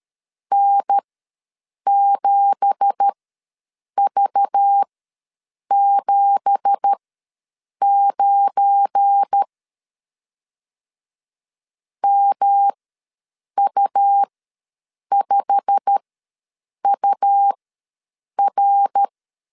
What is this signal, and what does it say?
Morse code, heard as a keyed tone, about 13 wpm, spaced as for 7 wpm, 800 Hz. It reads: N7V79 MU5UR
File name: LEZIONI / 9 2867 ws 12wpm 3431